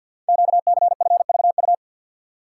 Morse code, heard as a keyed tone, about 50 wpm, 690 Hz. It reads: QCFVU
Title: Code Groups - 5 Letters - 50wpm